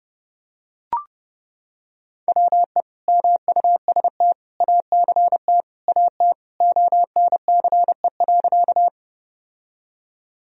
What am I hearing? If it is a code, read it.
Morse code, heard as a keyed tone, about 32 wpm, 700 Hz. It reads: WE MUST ACT AT ONCE.
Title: Sentences from Top 1000 Words - Review Part 1 - 30wpm